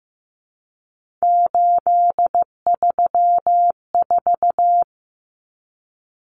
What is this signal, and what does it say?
Morse code, heard as a keyed tone, about 15 wpm, 700 Hz. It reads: 834
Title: Common Cut Numbers - 3-Digit 15wpm